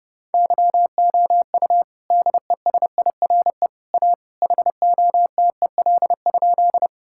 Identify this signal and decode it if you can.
Morse code, heard as a keyed tone, about 30 wpm, 700 Hz. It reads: YOU DESIRE A HOTEL?
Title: Sentences from Top 1600 Words 30wpm